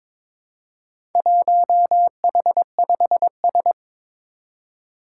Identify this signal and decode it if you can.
Morse code, heard as a keyed tone, about 22 wpm, 700 Hz. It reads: 1H5S
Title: Four Challenging Letter-Number Combinations 22wpm